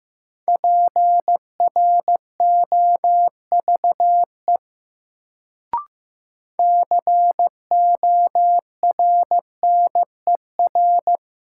Morse code, heard as a keyed tone, about 15 wpm, 700 Hz. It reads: PROVE CORNER
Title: Sentences from Top 1600 Words - New Word Drill Alt - 15wpm